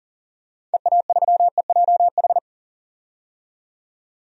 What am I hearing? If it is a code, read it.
Morse code, heard as a keyed tone, about 40 wpm, 700 Hz. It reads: EA3EJH
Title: DXCC-100 Call Signs - 6 Characters - 40wpm